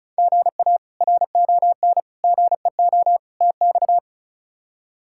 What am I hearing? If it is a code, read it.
Morse code, heard as a keyed tone, about 36 wpm, 700 Hz. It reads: GA RON GEO TX
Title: K1USN SST QSO Practice - 35wpm